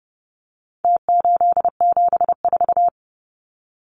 Morse code, heard as a keyed tone, about 30 wpm, 700 Hz. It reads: T874
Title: Numbers - Contest Style - 4-Digit 30wpm